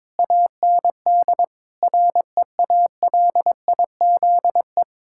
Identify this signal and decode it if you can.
Morse code, heard as a keyed tone, about 22 wpm, 700 Hz. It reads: AND REALIZE
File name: Difficult Sentences with Hints - 22wpm